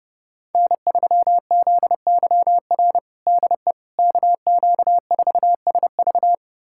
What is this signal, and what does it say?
Morse code, heard as a keyed tone, about 32 wpm, 700 Hz. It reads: N3ZYR DE KQ4SV
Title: USA Callsign DE Callsign - 5 Characters - 30wpm